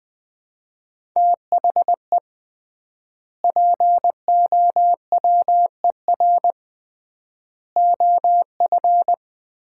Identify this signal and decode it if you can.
Morse code, heard as a keyed tone, about 20 wpm, 700 Hz. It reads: THE POWER OF
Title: 3x Word-Spacing - Sets of 4 Words - 20wpm